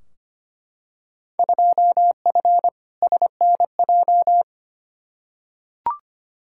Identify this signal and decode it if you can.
Morse code, heard as a keyed tone, about 25 wpm, 700 Hz. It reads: 2F SNJ
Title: Field Day Exchanges Encore - 25wpm